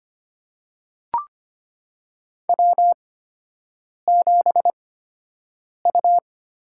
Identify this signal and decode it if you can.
Morse code, heard as a keyed tone, about 25 wpm, spaced as for 7 wpm, 700 Hz. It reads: W7U
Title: USA Call Signs - 4 Characters - 8x Character Spacing - 25wpm